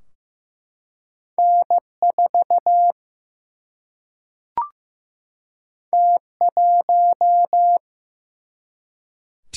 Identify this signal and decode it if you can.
Morse code, heard as a keyed tone, about 15 wpm, 700 Hz. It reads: N4 T1
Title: Single-Letter Single-Number 15wpm